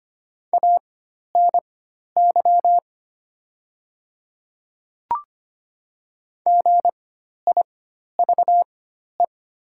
Morse code, heard as a keyed tone, about 25 wpm, spaced as for 12 wpm, 700 Hz. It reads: ANY GIVE
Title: 4x Character Spacing - Top 500 Words - 25wpm